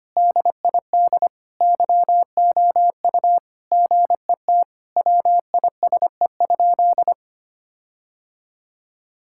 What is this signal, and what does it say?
Morse code, heard as a keyed tone, about 25 wpm, 700 Hz. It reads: DID YOU GET WISE?
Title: Sentences from Top 1600 Words 25wpm